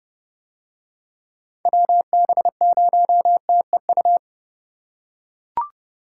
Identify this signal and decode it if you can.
Morse code, heard as a keyed tone, about 29 wpm, 700 Hz. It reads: WB0TEU